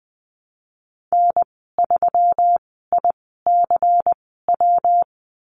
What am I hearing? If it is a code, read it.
Morse code, heard as a keyed tone, about 19 wpm, spaced as for 14 wpm, 700 Hz. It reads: N3ICW
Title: 167 Lesson 27g - Number 7 - Call Signs 20wpm